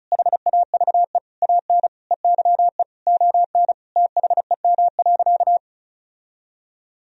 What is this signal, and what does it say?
Morse code, heard as a keyed tone, about 34 wpm, 700 Hz. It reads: HAVE AN EYE ON THEM.
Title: Sentences from Top 1000 Words - Review Part 1 - 35wpm